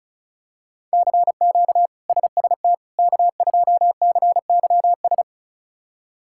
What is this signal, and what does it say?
Morse code, heard as a keyed tone, about 34 wpm, 700 Hz. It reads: CQ SST K2CYS
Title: K1USN SST QSO Practice - 35wpm